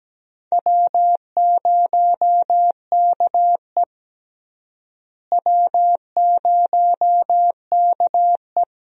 Morse code, heard as a keyed tone, about 17 wpm, 700 Hz. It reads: W0KE W0KE
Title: USA Call Signs - 4 Characters - Repeated 2x - 17wpm